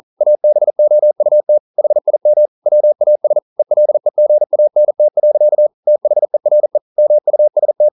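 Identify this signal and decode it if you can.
Morse code, heard as a keyed tone, about 42 wpm, 600 Hz. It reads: ABOUT HIM WAS ELEGANT. THERE MUST
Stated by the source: Morse code practice at 41 wpm, from CW med Gnister.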